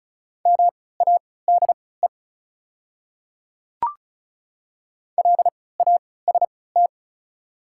Morse code, heard as a keyed tone, about 36 wpm, spaced as for 20 wpm, 700 Hz. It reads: MADE LAST
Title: 3x Character Spacing - Top 300 Words - 35wpm